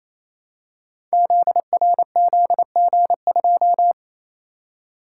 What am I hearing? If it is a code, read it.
Morse code, heard as a keyed tone, about 27 wpm, 700 Hz. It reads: ZRZG2
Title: Code Groups - 5 Letter:Numbers - 28wpm